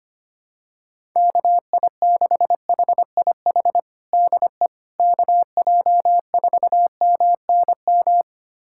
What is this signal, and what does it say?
Morse code, heard as a keyed tone, about 25 wpm, 700 Hz. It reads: KI6HIH DE KJ4MNM
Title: USA Callsign DE Callsign 25wpm